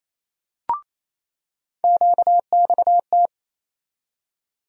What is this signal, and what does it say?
Morse code, heard as a keyed tone, about 28 wpm, 700 Hz. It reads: QXT
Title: USA Call Sign Postfixes - 3 Letters 28wpm